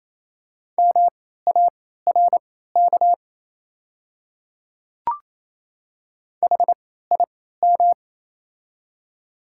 Morse code, heard as a keyed tone, about 27 wpm, spaced as for 16 wpm, 700 Hz. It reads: MARK HIM
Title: 3x Character Spacing - Top 300 Words - 28wpm